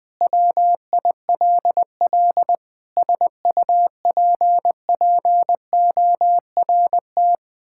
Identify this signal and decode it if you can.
Morse code, heard as a keyed tone, about 21 wpm, 700 Hz. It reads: WILL SUPPORT
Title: Sentences from Top 1000 Words - Review Part 5 - 20wpm